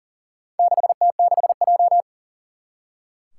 Morse code, heard as a keyed tone, about 40 wpm, 700 Hz. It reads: 6T6J
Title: Four Challenging Letter-Number Combinations 40wpm